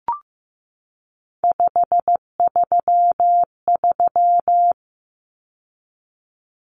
Morse code, heard as a keyed tone, about 15 wpm, 700 Hz. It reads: 533
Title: Numbers - 3-Digit 15wpm